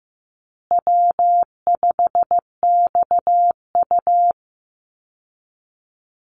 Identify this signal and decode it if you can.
Morse code, heard as a keyed tone, about 15 wpm, 700 Hz. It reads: W5XU
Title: Easy Call Signs Repeated 2x - 15wpm